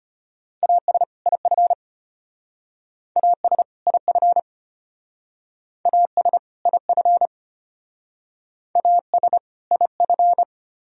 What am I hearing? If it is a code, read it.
Morse code, heard as a keyed tone, about 33 wpm, 700 Hz. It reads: AS IF AS IF AS IF AS IF